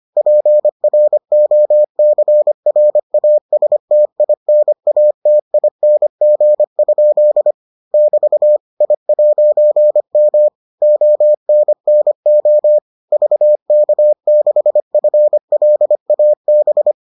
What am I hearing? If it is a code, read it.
Morse code, heard as a keyed tone, about 25 wpm, 600 Hz. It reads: PROCRASTINATING? <BT> I'M ONNO VK6FLAB